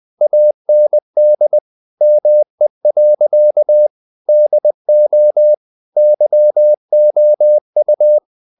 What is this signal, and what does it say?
Morse code, heard as a keyed tone, about 20 wpm, 600 Hz. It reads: AND ME. DO YOU
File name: the old man and the sea / the old man and the sea part 1 20wpm